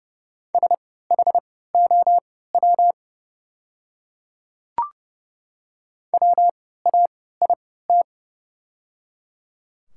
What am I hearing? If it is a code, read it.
Morse code, heard as a keyed tone, about 30 wpm, spaced as for 17 wpm, 690 Hz. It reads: SHOW WAIT
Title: 3x Character Spacing - Top 500 Words - 30wpm